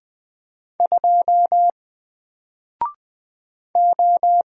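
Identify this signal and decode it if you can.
Morse code, heard as a keyed tone, about 20 wpm, 700 Hz. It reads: 2 O
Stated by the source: Single letter and number Morse drill at 20 wpm, from N4LCD.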